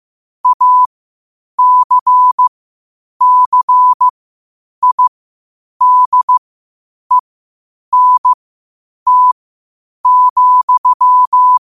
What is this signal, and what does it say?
Morse code, heard as a keyed tone, about 15 wpm, spaced as for 8 wpm, 1000 Hz. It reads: ACCIDENT,
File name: qotdfiles / 8wpm